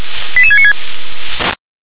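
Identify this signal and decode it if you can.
Five-tone selective call (ZVEI), heard as quick runs of five tones, tones of about 70 ms. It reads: 88696
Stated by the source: Five-tone ZVEI selective call received with an SDR.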